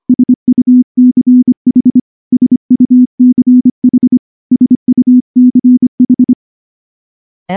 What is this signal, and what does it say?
Morse code, heard as a keyed tone, about 24 wpm, 250 Hz. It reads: SUCH SUCH SUCH
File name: morse-training-25WPM